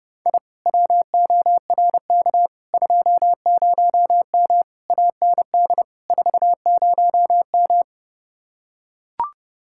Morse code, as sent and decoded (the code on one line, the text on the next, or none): .. / .-- --- .-. -.- / ..--- ----- -- / .- -. -.. / ....- ----- --
I WORK 20M AND 40M